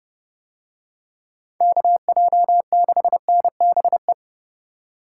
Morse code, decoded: KJ6NBE

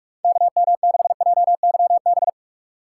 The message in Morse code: KM6JYB

